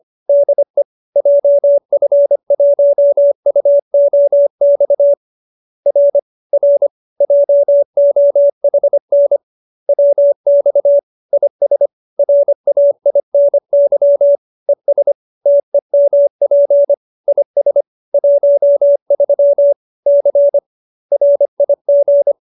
Morse code, heard as a keyed tone, about 25 wpm, 570 Hz. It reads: DE JF1UOX R R JOHN WX IS RAINY ES TEMP IS 13 C RIG